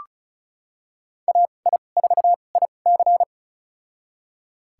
Morse code, heard as a keyed tone, about 34 wpm, spaced as for 25 wpm, 700 Hz. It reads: AI4IC
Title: USA Call Signs - 5 Characters - 2x Character Spacing - 35wpm